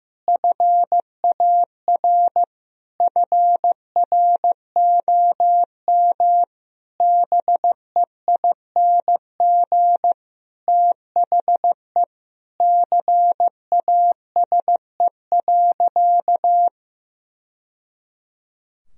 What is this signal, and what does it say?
Morse code, heard as a keyed tone, about 15 wpm, 700 Hz. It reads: FAR FROM BEING THE CASE.
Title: Sentences from Top 1000 Words - Review Part 3 - 15wpm